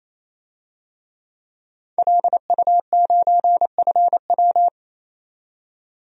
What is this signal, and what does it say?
Morse code, heard as a keyed tone, about 28 wpm, 700 Hz. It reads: LU9FW
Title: DXCC-100 Call Signs - 5 Characters - 28wpm